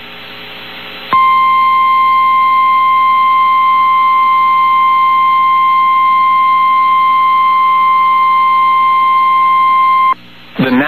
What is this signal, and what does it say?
Emergency Alert System (SAME) tones (the 1050 Hz attention tone). It is heard from 1.1 to 10.2 s.
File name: eas received alerts / L1 19455 audio